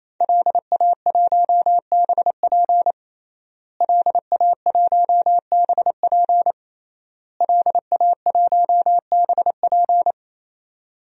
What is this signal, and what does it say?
Morse code, heard as a keyed tone, about 28 wpm, 690 Hz. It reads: LA1BP LA1BP LA1BP
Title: DXCC-100 Call Signs - 5 Characters - Repeated 3x - 28wpm